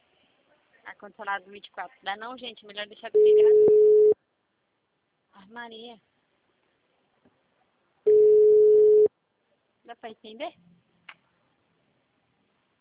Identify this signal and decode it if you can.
Ringback tone (Europe, 425 Hz), with speech.
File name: out- -2002- - - .1405